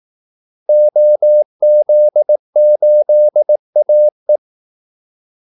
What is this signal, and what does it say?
Morse code, heard as a keyed tone, about 18 wpm, 600 Hz. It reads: OZ8AE
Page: Kaldesignaler 18 wpm